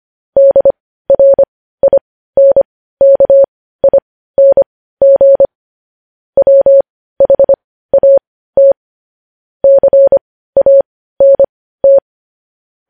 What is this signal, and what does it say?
Morse code, heard as a keyed tone, about 25 wpm, spaced as for 15 wpm, 550 Hz. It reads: DRINKING WHAT CANT
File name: podcast / News headlines-1x-25x15